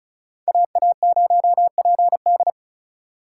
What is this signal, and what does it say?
Morse code, heard as a keyed tone, about 36 wpm, 700 Hz. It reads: AA0PD
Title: USA Call Signs - 5 Characters - 4x Character Spacing - 35wpm